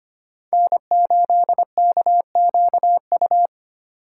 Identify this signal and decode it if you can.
Morse code, heard as a keyed tone, about 25 wpm, 700 Hz. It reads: N8KQU
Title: USA Call Signs - 5 Characters - 3x Character Spacing - 25wpm